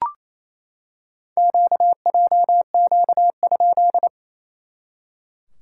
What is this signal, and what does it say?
Morse code, heard as a keyed tone, about 28 wpm, 690 Hz. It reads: QJQ?
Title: SOTA QSO Practice 28wpm